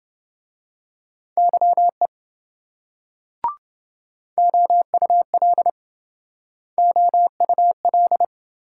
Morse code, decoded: YE OUL OUL

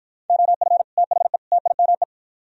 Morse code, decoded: OF THE TIME

